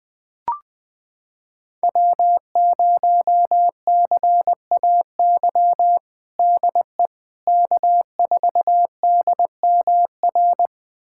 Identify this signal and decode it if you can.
Morse code, heard as a keyed tone, about 20 wpm, 700 Hz. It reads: W0CAY DE K4DMR
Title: USA Callsign DE Callsign - 5 Characters - 20wpm